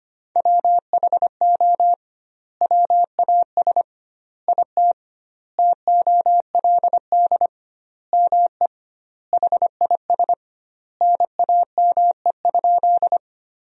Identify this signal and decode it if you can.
Morse code, heard as a keyed tone, about 25 wpm, 700 Hz. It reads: WHO WAS IT TOLD ME HIS NAME?
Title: 2x Word-Spacing - Sentences from Top 400 Words - 25wpm